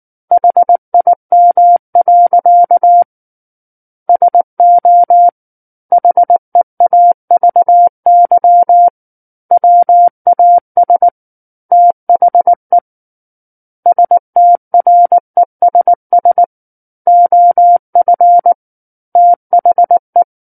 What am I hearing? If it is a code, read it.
Morse code, heard as a keyed tone, about 19 wpm, 700 Hz. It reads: HIM. SO HEAVY WAS THE STRESS OF THE
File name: Book One / 10-Chapter 10 - 19 WPM